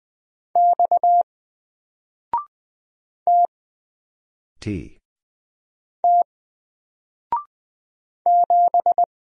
Morse code, heard as a keyed tone, about 20 wpm, 700 Hz. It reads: X T T 7
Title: Single Letter-Number 20wpm